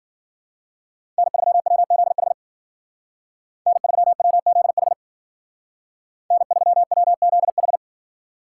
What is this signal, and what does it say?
Morse code, heard as a keyed tone, about 51 wpm, 700 Hz. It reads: N3WZH N3WZH N3WZH